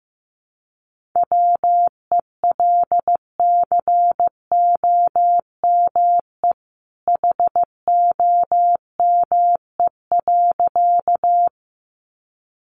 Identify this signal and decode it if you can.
Morse code, heard as a keyed tone, about 15 wpm, 700 Hz. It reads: WELCOME HOME.
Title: A Sound Of Thunder 15wpm